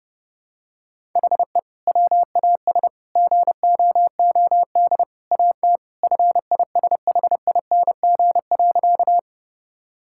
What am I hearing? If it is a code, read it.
Morse code, heard as a keyed tone, about 30 wpm, 700 Hz. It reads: HE WAS GOOD AT FISHING.
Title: Sentences from Top 1000 Words - Review Part 5 - 30wpm